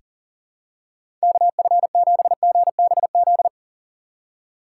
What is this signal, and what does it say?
Morse code, heard as a keyed tone, about 40 wpm, 700 Hz. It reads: KF7GBZ
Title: USA Call Signs - 6 Characters - 4x Character Spacing - 40wpm